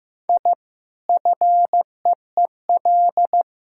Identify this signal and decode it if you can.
Morse code, heard as a keyed tone, about 16 wpm, 700 Hz. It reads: I FEEL
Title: Sentences from Top 1000 Words - Review Part 6 - 15wpm